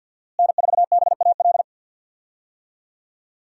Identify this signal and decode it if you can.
Morse code, heard as a keyed tone, about 48 wpm, 700 Hz. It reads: N4BAL
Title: USA Call Signs - 5 Characters - Repeated 3x - 50wpm